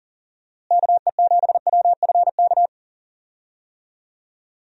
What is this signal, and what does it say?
Morse code, heard as a keyed tone, about 40 wpm, 700 Hz. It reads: KE7WFK
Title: USA Call Signs - 6 Characters - Repeated 2x - 40wpm